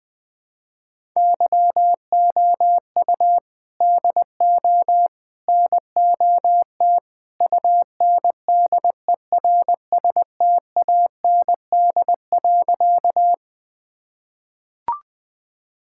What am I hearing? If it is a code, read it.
Morse code, heard as a keyed tone, about 19 wpm, 700 Hz. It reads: YOU DO NOT UNDERSTAND.
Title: Sentences with 3-Word Phrases - 20wpm